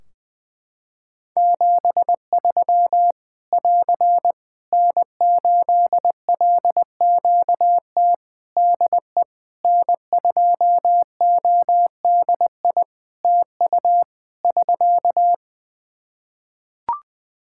Morse code, heard as a keyed tone, about 20 wpm, 700 Hz. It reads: 73 <AR> N8LQT DE N2ODI TU <SK>
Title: K1USN SST QSO Practice 20wpm